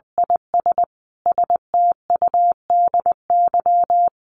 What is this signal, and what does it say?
Morse code, heard as a keyed tone, about 20 wpm, 700 Hz. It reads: IS STUDY
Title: Sets of 5 Words Encore2 20wpm